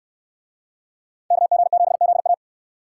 Morse code, heard as a keyed tone, about 69 wpm, 700 Hz. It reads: KG7ZA